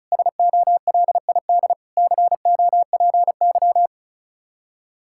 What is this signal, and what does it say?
Morse code, heard as a keyed tone, about 34 wpm, 700 Hz. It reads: SOLID COPY
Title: Practice QSO - Exchange 3 - More Info - 35wpm